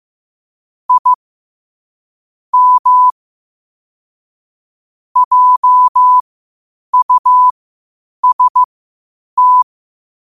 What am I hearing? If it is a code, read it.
Morse code, heard as a keyed tone, about 15 wpm, spaced as for 8 wpm, 1000 Hz. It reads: I M JUST
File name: qotdfiles / 8wpm